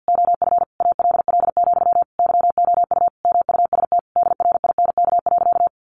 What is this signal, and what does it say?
Morse code, heard as a keyed tone, about 50 wpm, 700 Hz. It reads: OF ALL, YOU MUST DRINK.
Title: Sentences from Top 1500 Words 50wpm